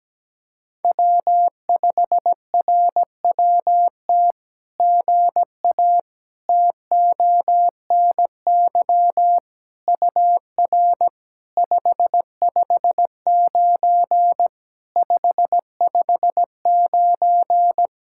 Morse code, heard as a keyed tone, about 17 wpm, 700 Hz. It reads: W5RWT GA TONY UR 559 559
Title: SOTA QSO Practice 17wpm